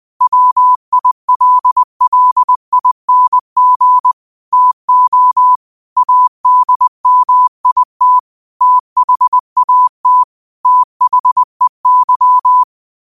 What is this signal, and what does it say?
Morse code, heard as a keyed tone, about 20 wpm, 1000 Hz. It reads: WILLING TO ADMIT THAT THEY